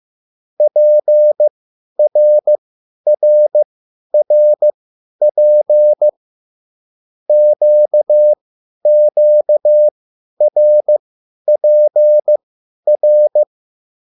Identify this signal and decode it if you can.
Morse code, heard as a keyed tone, about 15 wpm, spaced as for 10 wpm, 600 Hz. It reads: PRRRP QQRPR
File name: QSL / CW lesson